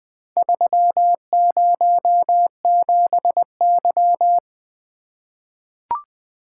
Morse code, heard as a keyed tone, about 20 wpm, 700 Hz. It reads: 307Y
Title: Four Letter-Number Combinations - 20wpm